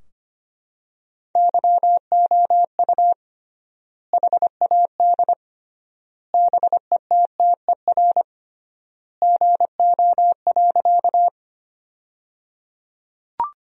Morse code, heard as a keyed tone, about 25 wpm, 700 Hz. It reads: YOU HAD BETTER GO.
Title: 3x Word-Spacing - Sentences from Top 400 Words - 25wpm